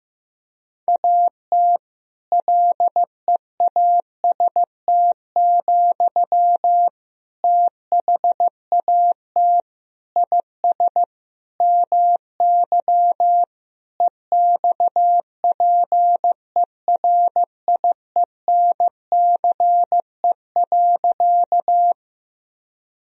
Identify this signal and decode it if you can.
Morse code, heard as a keyed tone, about 15 wpm, 700 Hz. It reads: AT LEAST, THAT IS MY EXPERIENCE.
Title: Sentences from Top 1000 Words - Review Part 3 - 15wpm